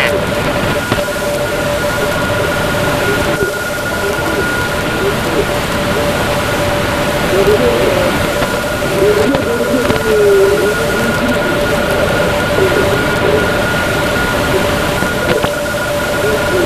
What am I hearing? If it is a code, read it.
Two-tone teleprinter signal, about 850 Hz shift.